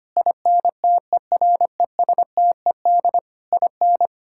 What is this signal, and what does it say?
Morse code, heard as a keyed tone, about 25 wpm, 700 Hz. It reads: INTERESTED IN
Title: Sentences from Top 1000 Words - Review Part 6 - 25wpm